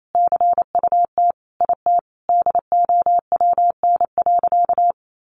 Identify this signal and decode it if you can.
Morse code, heard as a keyed tone, about 28 wpm, 700 Hz. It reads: CUT IT DOWN.
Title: Sentences from Top 1200 Words 28wpm